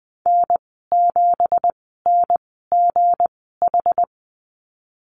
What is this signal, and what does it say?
Morse code, heard as a keyed tone, about 20 wpm, spaced as for 14 wpm, 700 Hz. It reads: N7NGH
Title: 193 Lesson 31g - Letter v - Call Signs 20wpm